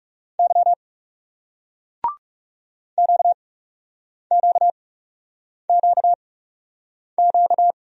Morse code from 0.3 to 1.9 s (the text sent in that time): Y